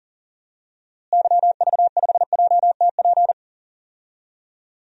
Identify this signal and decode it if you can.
Morse code, heard as a keyed tone, about 40 wpm, 700 Hz. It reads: YV5JTP
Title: DXCC-100 Call Signs - 6 Characters - Repeated 3x - 40wpm